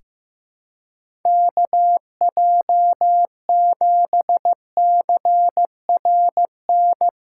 Morse code, heard as a keyed tone, about 15 wpm, 700 Hz. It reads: KJ7CRN
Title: USA Callsign DE Callsign - 6 Characters - 15wpm